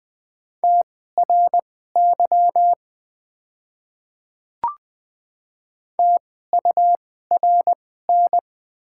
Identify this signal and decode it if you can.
Morse code, heard as a keyed tone, about 20 wpm, spaced as for 14 wpm, 700 Hz. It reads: TRY TURN
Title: 2x Character Spacing - Top 200 Words - 20wpm